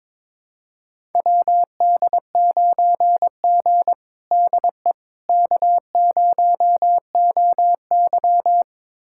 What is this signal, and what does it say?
Morse code, heard as a keyed tone, about 22 wpm, 700 Hz. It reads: WD9G DE K0OY
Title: USA Callsign DE Callsign - 4 Characters - 22wpm